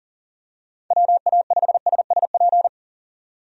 Morse code, heard as a keyed tone, about 37 wpm, 690 Hz. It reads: WA5SSP
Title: USA Call Signs - 6 Characters - 40wpm